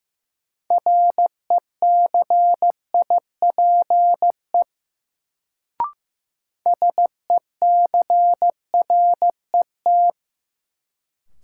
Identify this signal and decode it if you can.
Morse code, heard as a keyed tone, about 15 wpm, 700 Hz. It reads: RECIPE SECRET
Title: Sentences from Top 1200 Words - New Word Drill Alt - 15wpm